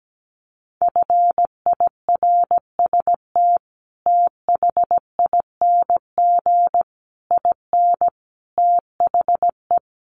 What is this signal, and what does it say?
Morse code, heard as a keyed tone, about 17 wpm, 700 Hz. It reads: FIRST THING IN THE
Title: Sets of 5 Words Encore2 17wpm